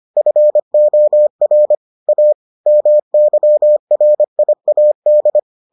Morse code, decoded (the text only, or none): FOR A MYRIAD